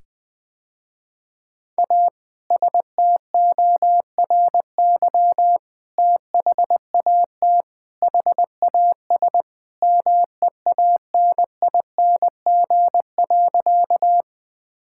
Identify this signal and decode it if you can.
Morse code, heard as a keyed tone, about 21 wpm, 700 Hz. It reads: A STORY THAT HAS MEANING.
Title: Sentences with 2-Word Phrases - Spoken Beginning - 20wpm